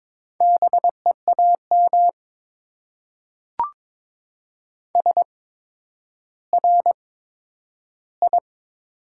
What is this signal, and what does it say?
Morse code, heard as a keyed tone, about 22 wpm, spaced as for 11 wpm, 700 Hz. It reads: BEAM SRI
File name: Top 90 QSO Elements - 8x Character Spacing - 22wpm